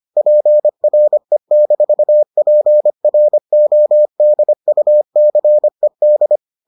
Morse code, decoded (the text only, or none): PRE-PRODUCED